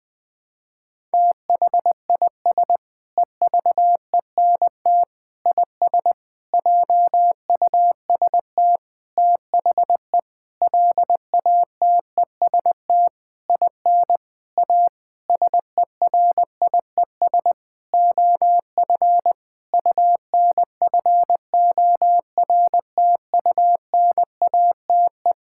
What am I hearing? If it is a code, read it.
Morse code, heard as a keyed tone, about 20 wpm, 700 Hz. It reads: THIS EVENT IS JUST THE LATEST IN A SERIES OF UNFORTUNATE
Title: Sentences with 8-Word Phrases - Spoken Beginning - 20wpm